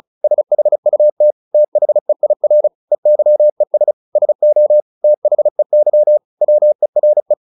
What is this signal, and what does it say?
Morse code, heard as a keyed tone, about 34 wpm, 600 Hz. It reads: SHUT THEIR EYES SO THEY WERE
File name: the old man and the sea / the old man and the sea 1 35wpm